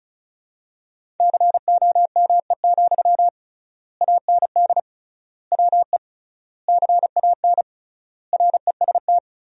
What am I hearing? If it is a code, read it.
Morse code, heard as a keyed tone, about 36 wpm, 700 Hz. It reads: COME, AND WE CAN REST
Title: 3x Word-Spacing - Sentences from Top 500 Words - 35wpm